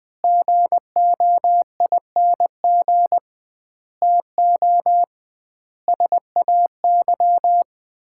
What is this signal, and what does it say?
Morse code, heard as a keyed tone, about 20 wpm, 700 Hz. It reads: GOING TO SAY
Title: 2x Word-Spacing - Sets of 5 Words - 20wpm